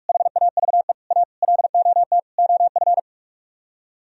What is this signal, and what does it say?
Morse code, heard as a keyed tone, about 49 wpm, 700 Hz. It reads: HAVE A LOT OF